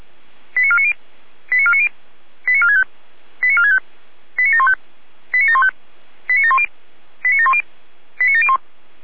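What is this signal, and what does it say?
Five-tone selective call (ZVEI), heard as quick runs of five tones, tones of about 70 ms. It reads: 89490 89490 89465 89465 89715 89715 89710 89710 89701
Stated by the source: ZVEI five-tone call 89533 received by SDR.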